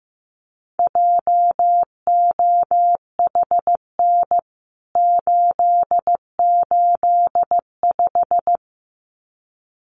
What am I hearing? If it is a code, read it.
Morse code, heard as a keyed tone, about 15 wpm, 700 Hz. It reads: JOHN 885
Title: CWT Exchange 15wpm